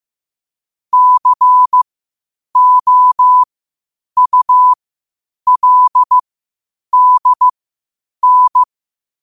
Morse code, heard as a keyed tone, about 15 wpm, spaced as for 8 wpm, 1000 Hz. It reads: COULDN
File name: qotdfiles / 8wpm